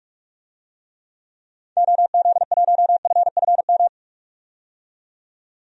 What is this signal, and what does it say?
Morse code, heard as a keyed tone, about 44 wpm, 700 Hz. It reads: OZ1FFM
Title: DXCC-100 Call Signs - 6 Characters - 45wpm